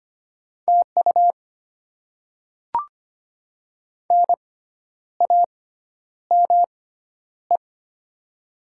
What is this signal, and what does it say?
Morse code, heard as a keyed tone, about 25 wpm, spaced as for 10 wpm, 700 Hz. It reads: TU NAME
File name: Top 20 QSO Elements - 6x Character Spacing - 25wpm